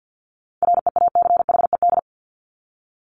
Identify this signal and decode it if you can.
Morse code, heard as a keyed tone, about 50 wpm, 700 Hz. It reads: REACHED
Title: ED Suffix - Top 100 Words with Base Words 50wpm